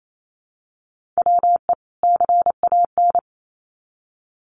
Morse code, heard as a keyed tone, about 28 wpm, 700 Hz. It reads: WE CAN
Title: Sets of 2 Words 28wpm